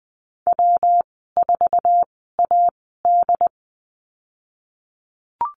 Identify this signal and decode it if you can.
Morse code, heard as a keyed tone, about 20 wpm, spaced as for 15 wpm, 700 Hz. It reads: W4AD
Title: 147 Lesson 24g - Letter y - Call Signs 20wpm